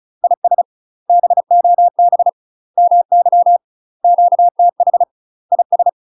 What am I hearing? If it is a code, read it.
Morse code, heard as a keyed tone, about 34 wpm, 700 Hz. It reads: IS BOB MY QTH IS
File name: UK MASTER QSOs TRAINING / 35WPM / UK MASTER QSOs 0039